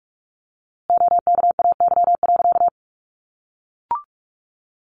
Morse code, heard as a keyed tone, about 44 wpm, 700 Hz. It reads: OKAY.
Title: A Sound Of Thunder 45wpm